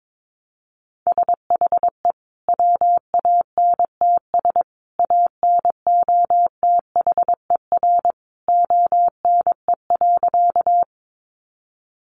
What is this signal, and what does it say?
Morse code, heard as a keyed tone, about 22 wpm, 700 Hz. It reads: SHE WANTS ANOTHER ONE.
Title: Easy Phrases 22wpm